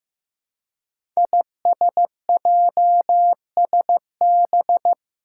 Morse code, heard as a keyed tone, about 15 wpm, 700 Hz. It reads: ISJSB
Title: Five Challenging Letter-Number Combinations 15wpm